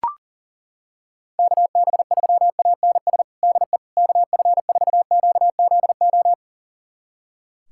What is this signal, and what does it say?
Morse code, heard as a keyed tone, about 37 wpm, 700 Hz. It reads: KB3ANS DE KF4QZO
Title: USA Callsign DE Callsign - 6 Characters - 40wpm